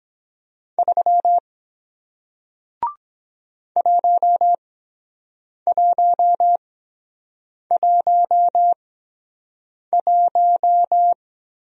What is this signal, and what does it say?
Morse code, heard as a keyed tone, about 23 wpm, spaced as for 7 wpm, 700 Hz. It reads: 3 1111